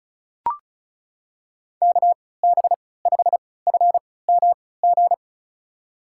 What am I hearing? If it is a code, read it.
Morse code, heard as a keyed tone, about 35 wpm, spaced as for 20 wpm, 700 Hz. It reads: KB5FMG
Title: USA Call Signs - 6 Characters - 3x Character Spacing - 35wpm